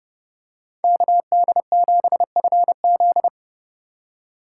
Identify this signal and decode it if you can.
Morse code, heard as a keyed tone, about 31 wpm, 700 Hz. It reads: KD7FZ
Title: USA Call Signs - 5 Characters - 4x Character Spacing - 30wpm